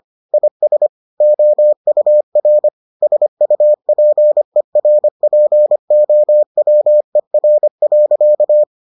Morse code, heard as a keyed tone, about 25 wpm, 600 Hz. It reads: IS OUR SUPERPOWER.